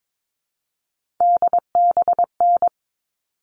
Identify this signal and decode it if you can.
Morse code, heard as a keyed tone, about 22 wpm, 700 Hz. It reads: DBN
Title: All Cut Numbers - 3-Digit 22wpm